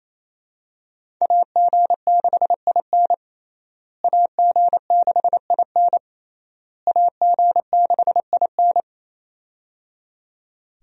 Morse code, heard as a keyed tone, about 28 wpm, 700 Hz. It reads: AG6IN AG6IN AG6IN
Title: USA Call Signs - 5 Characters - Repeated 3x - 28wpm